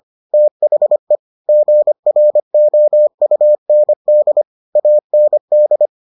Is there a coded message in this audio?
Yes: Morse code from 0.3 to 5.9 s, heard as a keyed tone, about 25 wpm, 600 Hz. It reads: THE GROUND AND